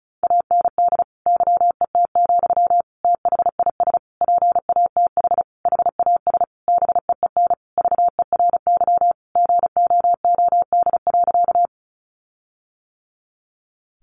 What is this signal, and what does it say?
Morse code, heard as a keyed tone, about 34 wpm, 700 Hz. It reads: AND YET, THIS PATH HAS BEEN VERY GOOD.
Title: Sentences from Top 1400 Words 35wpm